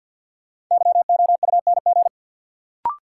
Morse code, heard as a keyed tone, about 53 wpm, 700 Hz. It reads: YOUNG
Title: Sentences from Top 400 Words - New Word Drill - 50wpm